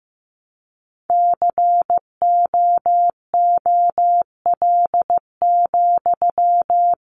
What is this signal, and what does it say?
Morse code, heard as a keyed tone, about 15 wpm, 700 Hz. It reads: COOL,
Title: Sentences from Top 1200 Words 15wpm